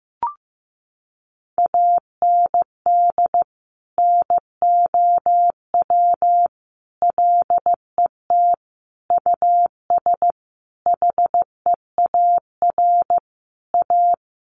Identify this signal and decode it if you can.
Morse code, heard as a keyed tone, about 14 wpm, 700 Hz. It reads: AND NOW LET US HEAR A
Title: Sentences from Top 900 Words 15wpm